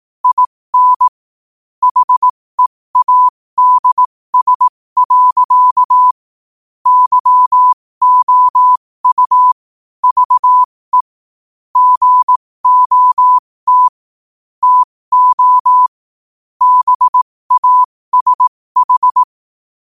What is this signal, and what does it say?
Morse code, heard as a keyed tone, about 18 wpm, spaced as for 15 wpm, 1000 Hz. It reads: IN HEADS. YOU VE GOT TO BASH
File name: qotdfiles / 15wpm